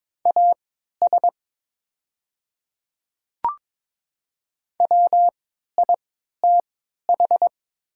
Morse code, heard as a keyed tone, about 22 wpm, spaced as for 13 wpm, 700 Hz. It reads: AS WITH
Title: 3x Character Spacing - Top 100 Words - 22wpm